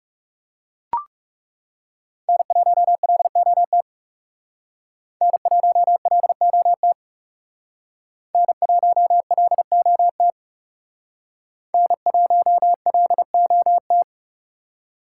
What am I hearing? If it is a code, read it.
Morse code, heard as a keyed tone, about 38 wpm, 700 Hz. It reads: N1LOT N1LOT N1LOT N1LOT